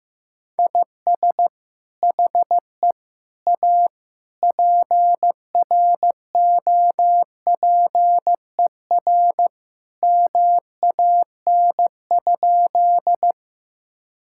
Morse code, heard as a keyed tone, about 15 wpm, 700 Hz. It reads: IS HE A PROPER MAN?
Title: Sentences from Top 1000 Words - Review Part 5 - 15wpm